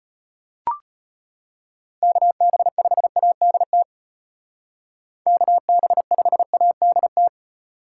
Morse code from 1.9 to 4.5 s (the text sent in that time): KB5ADT